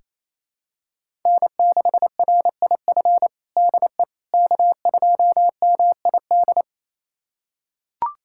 Morse code, heard as a keyed tone, about 27 wpm, 700 Hz. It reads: N6RIF DE K2MID
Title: USA Callsign DE Callsign - 5 Characters - 28wpm